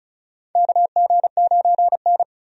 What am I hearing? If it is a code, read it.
Morse code, heard as a keyed tone, about 36 wpm, 700 Hz. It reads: KG9N
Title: USA Call Signs - 4 Characters - 35wpm